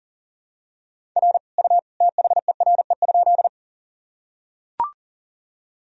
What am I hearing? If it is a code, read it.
Morse code, heard as a keyed tone, about 40 wpm, 700 Hz. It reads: R U THERE?
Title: Sentences from Top 100 Words and Abbreviations 40wpm